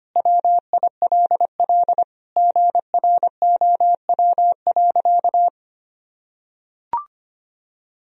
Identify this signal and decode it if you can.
Morse code, heard as a keyed tone, about 25 wpm, 700 Hz. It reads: WILL GROW.
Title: Sentences from Top 1000 Words - Review Part 2 - 25wpm